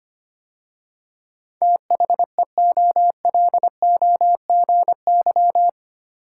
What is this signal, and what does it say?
Morse code, heard as a keyed tone, about 25 wpm, 700 Hz. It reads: THEOLOGY
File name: 8 Letter Words - First 3 Letters Given - 25wpm